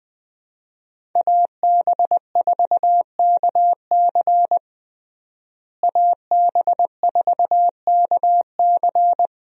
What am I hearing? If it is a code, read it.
Morse code, heard as a keyed tone, about 20 wpm, 700 Hz. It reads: AB4KC AB4KC
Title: USA Call Signs - 5 Characters - Repeated 2x - 20wpm